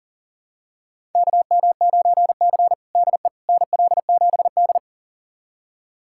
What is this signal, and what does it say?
Morse code, heard as a keyed tone, about 37 wpm, 690 Hz. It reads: KM9C DE NL7D
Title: USA Callsign DE Callsign - 4 Characters - 40wpm